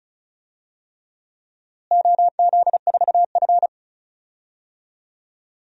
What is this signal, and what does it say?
Morse code, heard as a keyed tone, about 36 wpm, 700 Hz. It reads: OZ4F
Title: DXCC-100 Call Signs - 4 Characters - Repeated 3x - 35wpm